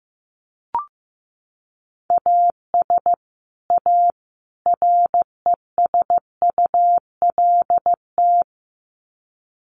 Morse code, heard as a keyed tone, about 15 wpm, 700 Hz. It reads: AS A RESULT
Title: Sets of 3 Words 15wpm